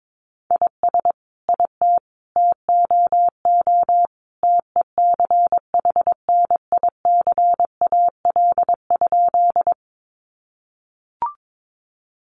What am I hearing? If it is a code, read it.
Morse code, heard as a keyed tone, about 22 wpm, 700 Hz. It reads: IS IT TOO TECHNICAL?
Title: Sentences from Top 1100 Words 22wpm